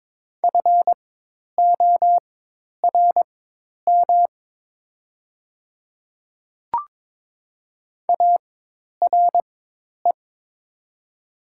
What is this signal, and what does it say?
Morse code, heard as a keyed tone, about 22 wpm, spaced as for 10 wpm, 700 Hz. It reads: FORM ARE
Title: 4x Character Spacing - Top 200 Words - 22wpm